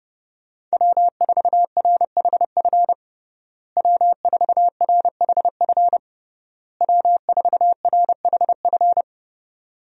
Morse code, heard as a keyed tone, about 30 wpm, 700 Hz. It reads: W4RHF W4RHF W4RHF
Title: USA Call Signs - 5 Characters - Repeated 3x - 30wpm